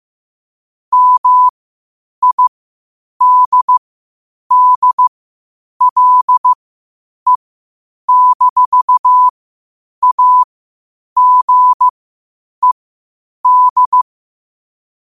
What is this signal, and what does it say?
Morse code, heard as a keyed tone, about 15 wpm, spaced as for 8 wpm, 1000 Hz. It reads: MIDDLE-AGED
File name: qotdfiles / 8wpm